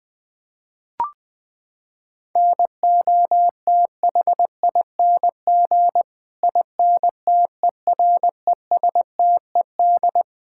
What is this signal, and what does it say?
Morse code, heard as a keyed tone, about 20 wpm, 700 Hz. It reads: NOTHING INTERESTED
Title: Sentences from Top 1000 Words - Review Part 6 - 20wpm